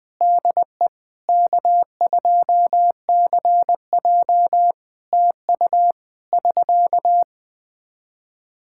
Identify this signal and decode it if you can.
Morse code, heard as a keyed tone, about 20 wpm, 700 Hz. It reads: DE K2CJ TU <SK>
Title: K1USN SST QSO Practice 20wpm